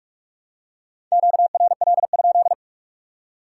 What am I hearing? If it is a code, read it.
Morse code, heard as a keyed tone, about 46 wpm, 700 Hz. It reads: QRL?